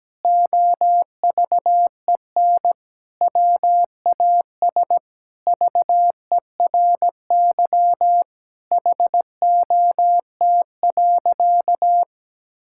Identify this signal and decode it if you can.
Morse code, heard as a keyed tone, about 17 wpm, 700 Hz. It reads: OVEN WAS VERY HOT.
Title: Sentences from Top 1000 Words - Review Part 6 - 17wpm